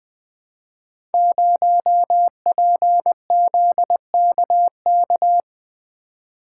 Morse code, heard as a keyed tone, about 20 wpm, 700 Hz. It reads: 0PZKK
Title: Code Groups - 5 Letter:Numbers - 20wpm